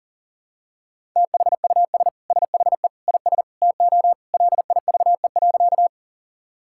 Morse code, heard as a keyed tone, about 37 wpm, 700 Hz. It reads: THUS SHE IS TO LIVE.
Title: Sentences from Top 1000 Words - Review Part 5 - 40wpm